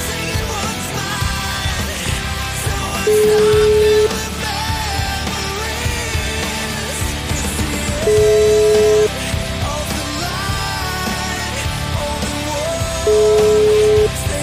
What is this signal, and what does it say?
Ringback tone (Europe, 425 Hz), with noise.